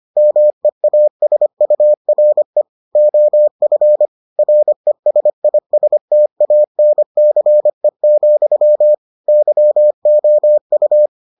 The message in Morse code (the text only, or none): MEASURE OF RESISTANCE, YOU